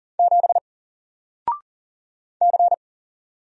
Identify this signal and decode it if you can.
Morse code, heard as a keyed tone, about 40 wpm, 700 Hz. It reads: 7 C
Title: Single Character (Letter:Number:Punctuation:Prosign) 40wpm